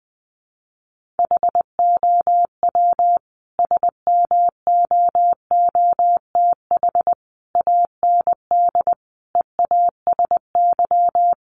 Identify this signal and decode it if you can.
Morse code, heard as a keyed tone, about 20 wpm, 700 Hz. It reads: HOW SMOOTH AND EASY
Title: Sentences from Top 1400 Words 20wpm